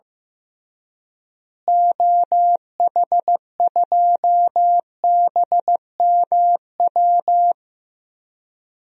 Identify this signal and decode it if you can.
Morse code, heard as a keyed tone, about 15 wpm, 700 Hz. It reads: OH2BMW
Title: DXCC-100 Call Signs - 6 Characters - Repeated 2x - 15wpm